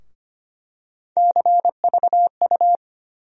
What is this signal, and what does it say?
Morse code, heard as a keyed tone, about 25 wpm, 700 Hz. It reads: CVU
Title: USA Call Sign Postfixes - 3 Letters 25wpm